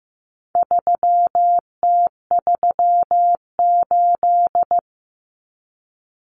Morse code, heard as a keyed tone, about 15 wpm, 700 Hz. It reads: 3T38
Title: Common Cut Numbers - 4-Digit 15wpm